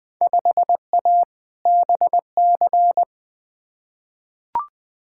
Morse code, heard as a keyed tone, about 21 wpm, 700 Hz. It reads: 5A BC
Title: Field Day Exchanges - 20wpm